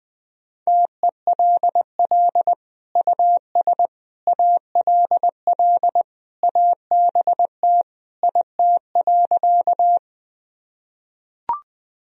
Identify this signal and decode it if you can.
Morse code, heard as a keyed tone, about 20 wpm, 700 Hz. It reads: TELL US ALL ABT IT.
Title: Sentences from Top 200 Words and Abbreviations 20wpm